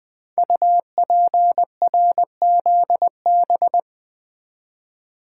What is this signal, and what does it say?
Morse code, heard as a keyed tone, about 20 wpm, 700 Hz. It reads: UPRZB
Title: Code Groups - 5 Letters - 20wpm